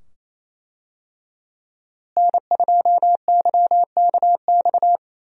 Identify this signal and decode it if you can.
Morse code, heard as a keyed tone, about 28 wpm, 700 Hz. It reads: N2YKX
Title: USA Call Signs - 5 Characters - Repeated 2x - 28wpm